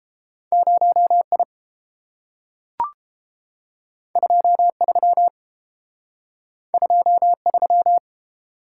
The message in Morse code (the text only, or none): 0 I 2 3 2 3